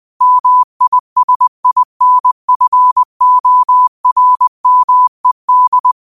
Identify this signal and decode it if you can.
Morse code, heard as a keyed tone, about 20 wpm, 1000 Hz. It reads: MISINFORMED
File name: 20wpm